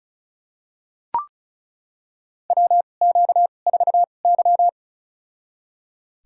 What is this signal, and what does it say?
Morse code, heard as a keyed tone, about 34 wpm, spaced as for 25 wpm, 700 Hz. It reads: WQ4Y
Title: USA Call Signs - 4 Characters - 2x Character Spacing - 35wpm